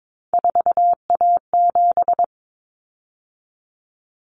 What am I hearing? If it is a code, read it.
Morse code, heard as a keyed tone, about 22 wpm, 700 Hz. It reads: 4A7
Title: Common Cut Numbers - 3-Digit 22wpm